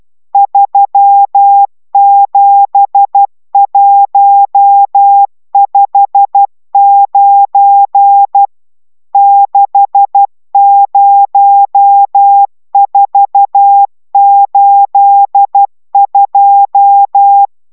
Morse code, heard as a keyed tone, about 12 wpm, 800 Hz. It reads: 37159 60482